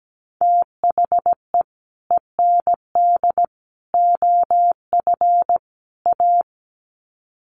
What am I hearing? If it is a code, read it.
Morse code, heard as a keyed tone, about 17 wpm, 700 Hz. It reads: THE END OF A
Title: Sets of 4 Words 17wpm